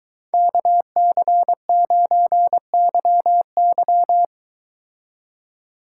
Morse code, heard as a keyed tone, about 23 wpm, 700 Hz. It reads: KC9YY